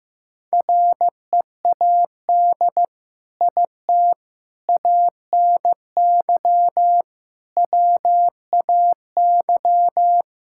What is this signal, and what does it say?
Morse code, heard as a keyed tone, about 15 wpm, 700 Hz. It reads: READ IT ANY WAY
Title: Sentences from Top 1600 Words 15wpm